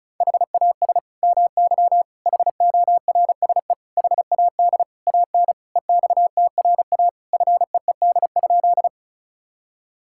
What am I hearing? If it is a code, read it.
Morse code, heard as a keyed tone, about 34 wpm, 700 Hz. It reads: HAS MY HORSE HAD AN EXTRA FEED?
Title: Sentences from Top 1600 Words 35wpm